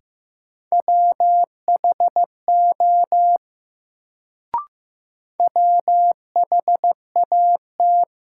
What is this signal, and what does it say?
Morse code, heard as a keyed tone, about 15 wpm, 700 Hz. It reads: WHO WHAT
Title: Sentences from Top 100 Words - New Word Drill - 15wpm